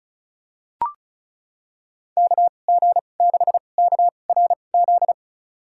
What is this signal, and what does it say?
Morse code, heard as a keyed tone, about 36 wpm, spaced as for 25 wpm, 700 Hz. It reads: KG6KRZ
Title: USA Call Signs - 6 Characters - 2x Character Spacing - 35wpm